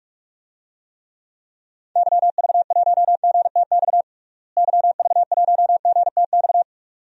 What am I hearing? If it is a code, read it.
Morse code, heard as a keyed tone, about 41 wpm, 700 Hz. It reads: YV1GTX YV1GTX
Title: DXCC-100 Call Signs - 6 Characters - Repeated 2x - 45wpm